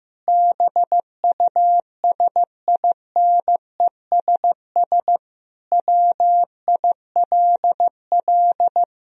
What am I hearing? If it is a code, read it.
Morse code, heard as a keyed tone, about 16 wpm, 700 Hz. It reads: BUSINESS WILL
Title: Sentences from Top 1000 Words - Review Part 4 - 15wpm